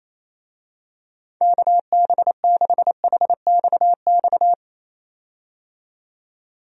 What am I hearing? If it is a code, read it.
Morse code, heard as a keyed tone, about 28 wpm, 700 Hz. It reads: KB6HXX
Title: USA Call Signs - 6 Characters - 28wpm